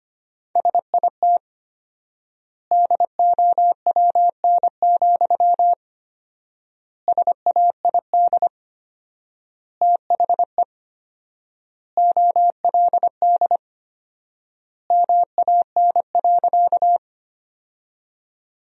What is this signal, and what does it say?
Morse code, heard as a keyed tone, about 25 wpm, 700 Hz. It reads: SIT DOWN, SAID THE OLD MAN.
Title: 4x Word-Spacing - Sentences from Top 500 Words - 25wpm